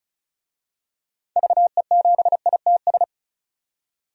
Morse code, heard as a keyed tone, about 34 wpm, 700 Hz. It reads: VE7ITS
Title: DXCC-100 Call Signs - 6 Characters - 35wpm